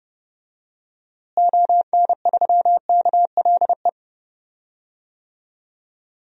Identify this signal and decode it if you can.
Morse code, heard as a keyed tone, about 30 wpm, 700 Hz. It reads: ON3KLE
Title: DXCC-100 Call Signs - 6 Characters - Repeated 2x - 30wpm